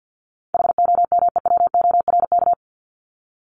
Morse code, heard as a keyed tone, about 50 wpm, 700 Hz. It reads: HOMEWORK
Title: Top 1501-1750 Words 50wpm